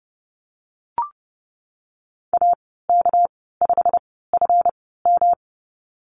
Morse code, heard as a keyed tone, about 30 wpm, spaced as for 17 wpm, 700 Hz. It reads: AK5FM
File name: Lesson 37g - Letter x - Call Signs 30wpm